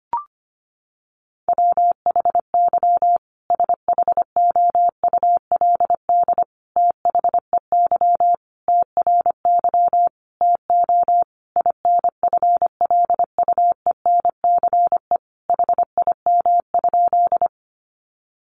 Morse code, heard as a keyed tone, about 25 wpm, 700 Hz. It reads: WHY SHOULD THEY TRY TO INFLUENCE HIM?
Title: Sentences from Top 1100 Words 25wpm